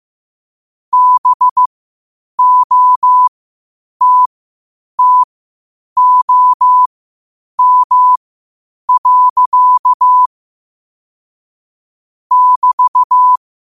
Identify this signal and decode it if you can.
Morse code, heard as a keyed tone, about 15 wpm, spaced as for 8 wpm, 1000 Hz. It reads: BOTTOM. <BT>
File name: qotdfiles / 8wpm